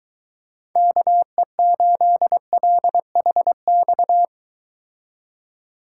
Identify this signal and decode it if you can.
Morse code, heard as a keyed tone, about 24 wpm, 700 Hz. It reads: KE8LHX